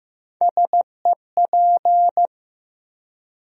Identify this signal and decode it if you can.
Morse code, heard as a keyed tone, about 15 wpm, 700 Hz. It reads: SEP
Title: USA Call Sign Postfixes - 3 Letters 15wpm